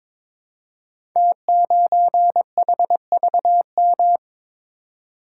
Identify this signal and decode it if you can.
Morse code, heard as a keyed tone, about 22 wpm, 700 Hz. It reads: T9HVM
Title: Code Groups - 5 Letter:Numbers - 22wpm